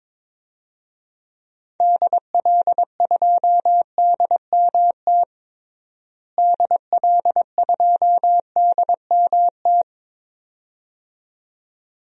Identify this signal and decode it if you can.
Morse code, heard as a keyed tone, about 22 wpm, 700 Hz. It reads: DL2DMT DL2DMT
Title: DXCC-100 Call Signs - 6 Characters - Repeated 2x - 22wpm